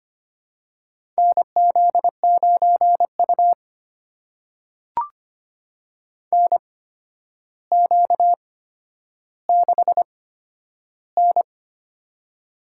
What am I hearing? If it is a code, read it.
Morse code, heard as a keyed tone, about 25 wpm, spaced as for 11 wpm, 700 Hz. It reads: NZ9U NQ6N
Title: USA Call Signs - 8x Character Spacing - 25wpm